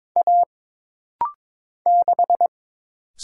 Morse code, heard as a keyed tone, about 22 wpm, 700 Hz. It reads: A 6
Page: This practice set is composed of Letters, Numbers, common punctuations, and common prosigns.